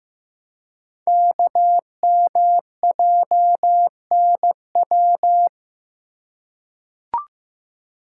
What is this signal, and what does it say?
Morse code, heard as a keyed tone, about 15 wpm, 700 Hz. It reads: KMJNW
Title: Code Groups - 5 Letters - 15wpm